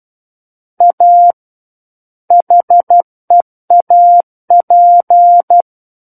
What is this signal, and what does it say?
Morse code, heard as a keyed tone, about 12 wpm, 700 Hz. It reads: A HEAP
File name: Book One / 03-Chapter 3 - 12 WPM